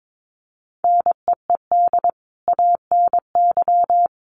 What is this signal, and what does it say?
Morse code, heard as a keyed tone, about 22 wpm, 700 Hz. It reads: NEED ANY
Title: Sentences from Top 1200 Words 22wpm